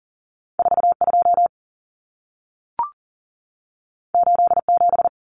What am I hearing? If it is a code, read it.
Morse code, heard as a keyed tone, about 40 wpm, 700 Hz. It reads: 42 87
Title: Common Cut Numbers - 2-Digit 40wpm